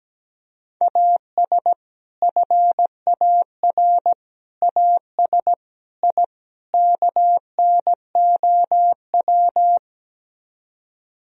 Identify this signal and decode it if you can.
Morse code, heard as a keyed tone, about 17 wpm, 700 Hz. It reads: AS FAR AS I KNOW
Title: Speed-Racing - Sets of 5 Words 17wpm